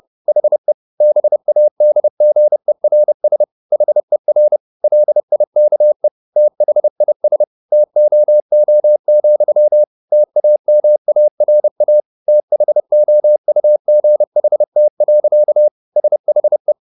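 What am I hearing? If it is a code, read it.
Morse code, heard as a keyed tone, about 30 wpm, 600 Hz. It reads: HE BADGERS HER LIKE THIS TOO, TAMARA THOUGHT. SHE
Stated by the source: Morse code at 30 wpm, the novel Never by Ken Follett in English, from CW med Gnister.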